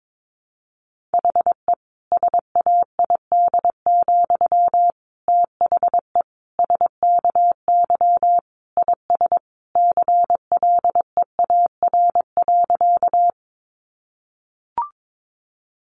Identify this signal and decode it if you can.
Morse code, heard as a keyed tone, about 22 wpm, 700 Hz. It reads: HE SAID, THE SKY IS CLEAR.
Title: Sentences from Top 1200 Words 22wpm